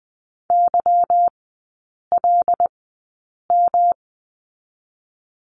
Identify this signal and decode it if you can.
Morse code, heard as a keyed tone, about 20 wpm, 700 Hz. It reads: Y L M
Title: 184 Lesson 30e - Letter b - Three Characters 20wpm